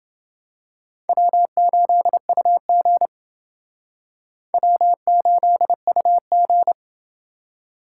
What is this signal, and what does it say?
Morse code, heard as a keyed tone, about 29 wpm, 700 Hz. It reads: W8UG W8UG